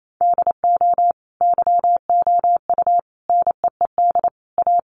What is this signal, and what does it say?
Morse code, heard as a keyed tone, about 28 wpm, 700 Hz. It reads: DO YOU NEED A
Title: Sentences from Top 1500 Words 28wpm